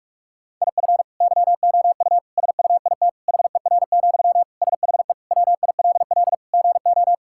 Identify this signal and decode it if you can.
Morse code, heard as a keyed tone, about 45 wpm, 700 Hz. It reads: IF YOU SUIT HER, SHE WILL GO
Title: Sentences from Top 1600 Words 45wpm